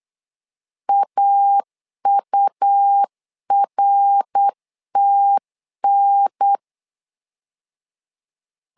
Morse code, heard as a keyed tone, about 8 wpm, 800 Hz. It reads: AURTN
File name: LEZIONI / 3 MWRO / 3 MWR 8wpm 2444